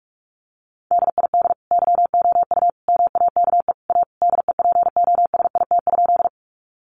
Morse code, heard as a keyed tone, about 44 wpm, 700 Hz. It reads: DID YOU MAKE A DEPOSIT?
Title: Sentences from Top 1300 Words 45wpm